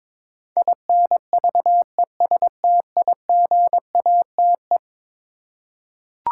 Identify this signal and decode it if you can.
Morse code, heard as a keyed tone, about 22 wpm, 700 Hz. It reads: INVESTIGATE
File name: practice / 22wpm / Eleven Letter Words - 22wpm